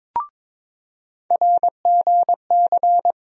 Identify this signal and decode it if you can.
Morse code, heard as a keyed tone, about 22 wpm, 700 Hz. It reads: RGC
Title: USA Call Sign Postfixes - 3 Letters 22wpm